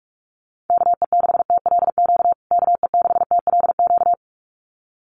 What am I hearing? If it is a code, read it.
Morse code, heard as a keyed tone, about 44 wpm, 700 Hz. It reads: KE6TLQ KE6TLQ
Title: USA Call Signs Repeated 2x 45wpm